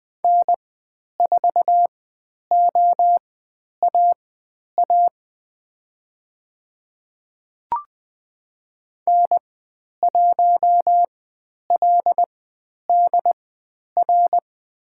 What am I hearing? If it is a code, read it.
Morse code, heard as a keyed tone, about 20 wpm, spaced as for 10 wpm, 700 Hz. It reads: N4OAA N1LDR
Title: 102 Lesson 17g - Letter u - Call Signs 20wpm